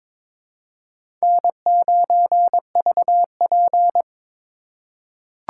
Morse code, heard as a keyed tone, about 22 wpm, 700 Hz. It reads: N9VP
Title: USA Call Signs - 4 Characters - 6x Character Spacing - 22wpm